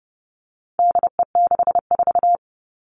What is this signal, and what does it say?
Morse code, heard as a keyed tone, about 30 wpm, 700 Hz. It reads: DE64
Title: All Cut Numbers - 4-Digit 30wpm